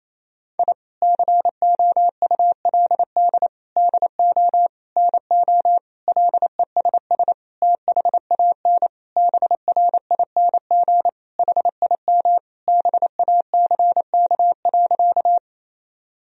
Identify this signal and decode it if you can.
Morse code, heard as a keyed tone, about 28 wpm, 700 Hz. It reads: I COULD DO NO LESS THAN BRING HIM BACK.
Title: Sentences from Top 1000 Words - Review Part 3 - 28wpm